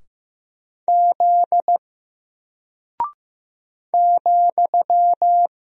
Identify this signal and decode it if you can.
Morse code, heard as a keyed tone, about 15 wpm, 700 Hz. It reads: Z ,
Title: Single Character (Letter:Number:Punctuation:Prosign) Alt 15wpm